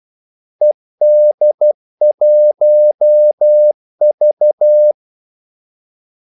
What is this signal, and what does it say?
Morse code, heard as a keyed tone, about 12 wpm, 600 Hz. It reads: ED1V
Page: 2 ekstra ordmellemrum mellem kaldesignalerne